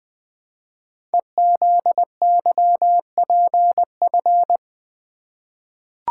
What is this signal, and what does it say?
Morse code, heard as a keyed tone, about 20 wpm, 700 Hz. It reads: EZYPF
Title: Code Groups - 5 Letters - 20wpm